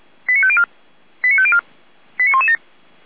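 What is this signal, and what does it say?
Five-tone selective call (ZVEI), heard as quick runs of five tones, tones of about 70 ms. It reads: 89504 89504 89117
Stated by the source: ZVEI five-tone selective call received via SDR.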